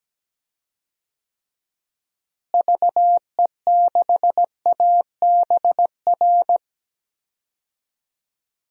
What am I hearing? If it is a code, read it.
Morse code, heard as a keyed tone, about 17 wpm, 700 Hz. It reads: VE6ABR
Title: DXCC-100 Call Signs - 6 Characters - Repeated 2x - 17wpm